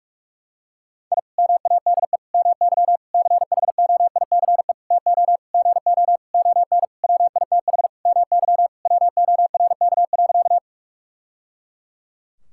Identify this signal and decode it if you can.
Morse code, heard as a keyed tone, about 46 wpm, 700 Hz. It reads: I MADE MY CHOICE TO GO ON WITH MY WORK.
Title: Sentences from Top 1000 Words - Review Part 6 - 45wpm